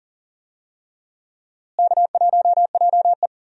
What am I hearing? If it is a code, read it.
Morse code, heard as a keyed tone, about 39 wpm, 700 Hz. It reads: K1JE
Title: USA Call Signs - 4 Characters - Repeated 3x - 40wpm